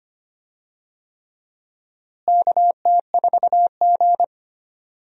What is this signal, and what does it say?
Morse code, heard as a keyed tone, about 25 wpm, 700 Hz. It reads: KT4G
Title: USA Call Signs - 4 Characters - Repeated 3x - 25wpm